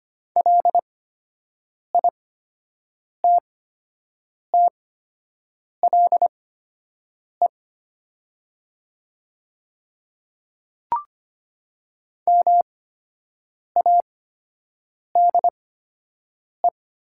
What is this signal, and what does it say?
Morse code, heard as a keyed tone, about 25 wpm, spaced as for 7 wpm, 700 Hz. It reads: LITTLE MADE
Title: 8x Character Spacing - Top 300 Words - 25wpm